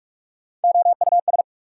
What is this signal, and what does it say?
Morse code, heard as a keyed tone, about 59 wpm, spaced as for 52 wpm, 690 Hz. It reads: OUS
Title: Speed-Racing Prefixes and Suffixes 45wpm